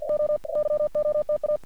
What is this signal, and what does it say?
Morse code, heard as a keyed tone, about 50 wpm, spaced as for 43 wpm, 600 Hz. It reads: OYOTA
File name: archived / CRSnet